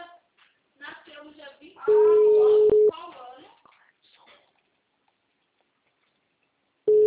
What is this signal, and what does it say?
Ringback tone (Europe, 425 Hz), with speech.
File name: out- -2032- - - .61597